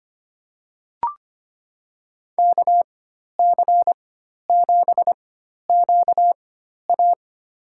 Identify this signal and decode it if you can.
Morse code, heard as a keyed tone, about 25 wpm, spaced as for 12 wpm, 700 Hz. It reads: KC7QA
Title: USA Call Signs - 5 Characters - 4x Character Spacing - 25wpm